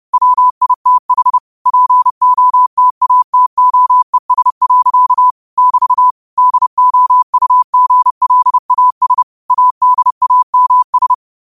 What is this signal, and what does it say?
Morse code, heard as a keyed tone, about 30 wpm, 1000 Hz. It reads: WITH POTATOES. <BT> DOUGLAS ADAMS
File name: qotdfiles / 30wpm